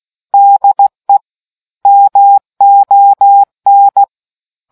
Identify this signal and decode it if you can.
Morse code, heard as a keyed tone, about 16 wpm, 800 Hz. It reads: DE MON